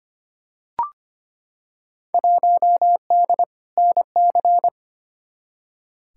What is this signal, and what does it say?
Morse code, heard as a keyed tone, about 25 wpm, 700 Hz. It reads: 1D NC
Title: Field Day Exchanges - 25wpm